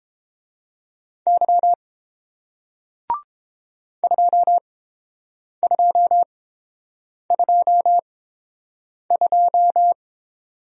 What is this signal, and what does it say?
Morse code, heard as a keyed tone, about 30 wpm, spaced as for 7 wpm, 700 Hz. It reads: Y 2222